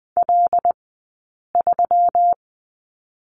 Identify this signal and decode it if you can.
Morse code, heard as a keyed tone, about 20 wpm, 700 Hz. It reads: L 3
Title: 177 Lesson 29d - Slash - Two Characters 20wpm